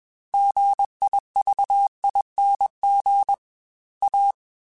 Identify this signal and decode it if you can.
Morse code, heard as a keyed tone, about 21 wpm, 800 Hz. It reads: GIVING A